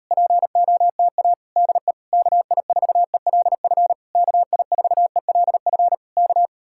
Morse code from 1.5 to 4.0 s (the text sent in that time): DE KI4ELF